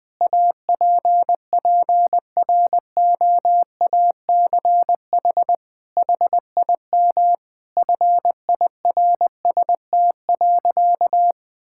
Morse code, heard as a keyed tone, about 20 wpm, 700 Hz. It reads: APPROACH HIM FIRST.
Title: Sentences from Top 1000 Words - Review Part 6 - 20wpm